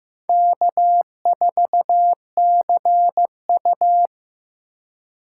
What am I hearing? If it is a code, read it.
Morse code, heard as a keyed tone, about 15 wpm, 700 Hz. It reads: K4CU
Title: USA Call Signs - 4 Characters - 4x Character Spacing - 15wpm